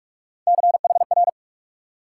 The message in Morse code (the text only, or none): CHR